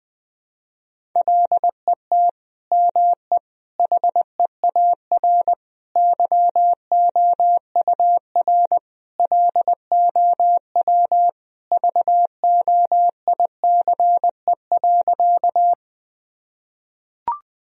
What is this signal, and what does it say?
Morse code, heard as a keyed tone, about 20 wpm, 700 Hz. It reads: LET ME HEAR YOUR LOW VOICE.
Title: Sentences from Top 1000 Words - Review Part 1 - 20wpm